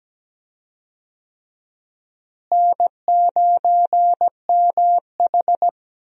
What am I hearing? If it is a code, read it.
Morse code, heard as a keyed tone, about 17 wpm, 700 Hz. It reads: N9MH
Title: USA Call Signs - 4 Characters - Repeated 3x - 17wpm